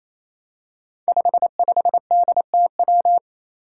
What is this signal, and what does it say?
Morse code, heard as a keyed tone, about 29 wpm, 700 Hz. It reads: 55DTW
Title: Five Challenging Letter-Number Combinations 28wpm